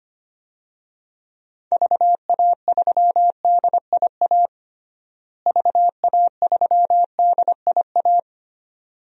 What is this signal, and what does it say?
Morse code, heard as a keyed tone, about 25 wpm, 700 Hz. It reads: VA3DIA VA3DIA
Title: DXCC-100 Call Signs - 6 Characters - Repeated 2x - 25wpm